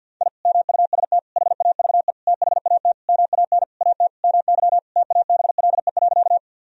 Morse code, heard as a keyed tone, about 53 wpm, 700 Hz. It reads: I MUST HAVE THAT MAN AT MY TABLE.
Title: Sentences from Top 1000 Words - Review Part 2 - 50wpm